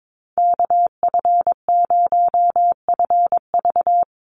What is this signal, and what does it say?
Morse code, heard as a keyed tone, about 22 wpm, 700 Hz. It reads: KF0FV
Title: USA Call Signs Repeated 3x 22wpm